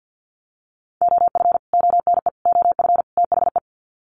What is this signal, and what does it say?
Morse code, heard as a keyed tone, about 50 wpm, 700 Hz. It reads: OF ONE OF THE
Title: Sets of 4 Words 50wpm